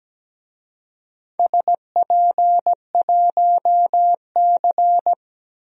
Morse code, heard as a keyed tone, about 17 wpm, 700 Hz. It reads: SP1C
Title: DXCC-100 Call Signs - 4 Characters - 17wpm